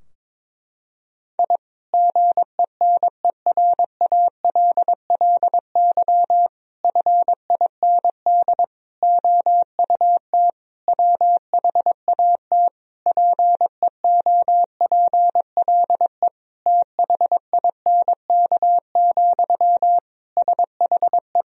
Morse code, heard as a keyed tone, about 22 wpm, 700 Hz. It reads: I GENERALLY FIND OUT WHAT PEOPLE THINK, SHE
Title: Sentences from Top 1000 Words - Review Part 4 - 22wpm